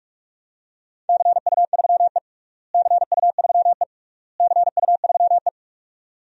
Morse code, heard as a keyed tone, about 41 wpm, 700 Hz. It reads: CU3E CU3E CU3E
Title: DXCC-100 Call Signs - 4 Characters - Repeated 3x - 45wpm